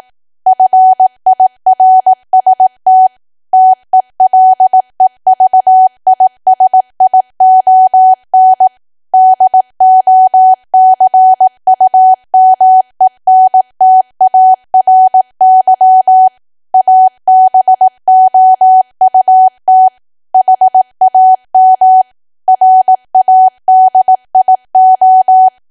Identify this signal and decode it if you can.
Morse code, heard as a keyed tone, about 18 wpm, 750 Hz. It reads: FIRST TELEVISION DOCUMENTARY ABOUT HAM RADIO
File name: Archive / 18 WPM / 18WPM